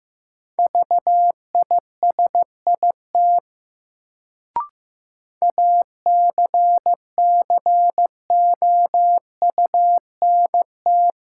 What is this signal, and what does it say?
Morse code, heard as a keyed tone, about 15 wpm, 700 Hz. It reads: VISIT ACCOUNT
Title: Sentences from Top 900 Words - New Word Drill - 15wpm